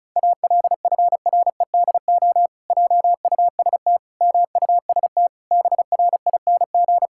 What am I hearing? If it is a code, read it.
Morse code, heard as a keyed tone, about 36 wpm, 700 Hz. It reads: ALFREDO JUST MUST BRING
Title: Pangram Challenge v2 - 35wpm